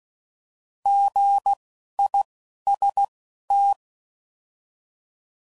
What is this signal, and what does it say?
Morse code, heard as a keyed tone, about 16 wpm, spaced as for 11 wpm, 800 Hz. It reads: GIST